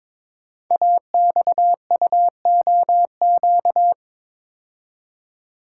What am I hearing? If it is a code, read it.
Morse code, heard as a keyed tone, about 22 wpm, 700 Hz. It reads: AXUOQ
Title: Code Groups - 5 Letters - 22wpm